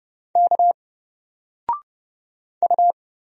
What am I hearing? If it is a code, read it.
Morse code, heard as a keyed tone, about 30 wpm, 700 Hz. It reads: K U
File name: Single-Letter-Number-30wpm